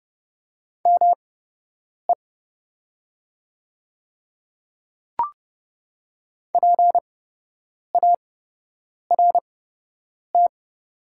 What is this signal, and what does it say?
Morse code, heard as a keyed tone, about 30 wpm, spaced as for 8 wpm, 700 Hz. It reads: ME PART
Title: 8x Character Spacing - Top 200 Words - 30wpm